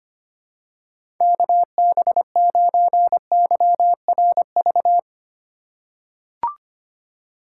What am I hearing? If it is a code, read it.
Morse code, heard as a keyed tone, about 25 wpm, 700 Hz. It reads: KB9YRV
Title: USA Call Signs - 6 Characters - 4x Character Spacing - 25wpm